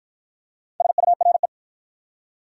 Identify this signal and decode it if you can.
Morse code, heard as a keyed tone, about 54 wpm, 700 Hz. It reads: SURE